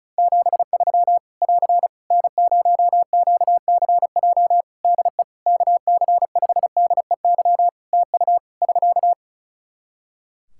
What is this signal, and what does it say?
Morse code, heard as a keyed tone, about 34 wpm, 700 Hz. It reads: 73 <AR> N0QCJ DE KC5DEY TU <SK>
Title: Practice QSO - Exchange 4 - Ending - 35wpm